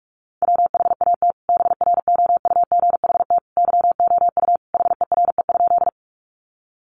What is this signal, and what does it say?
Morse code, heard as a keyed tone, about 44 wpm, 700 Hz. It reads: WHAT BROUGHT YOU HERE?
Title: Sentences from Top 500 Words 45wpm